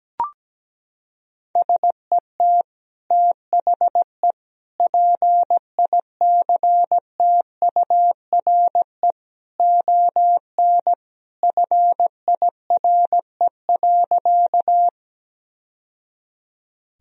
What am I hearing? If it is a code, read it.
Morse code, heard as a keyed tone, about 17 wpm, 700 Hz. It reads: SET THE PICTURE ON FIRE.
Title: Sentences from Top 1000 Words - Review Part 1 - 17wpm